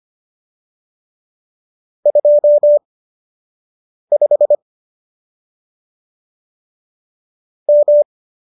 Morse code, heard as a keyed tone, about 25 wpm, spaced as for 6 wpm, 570 Hz. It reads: 25 M
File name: qso003 06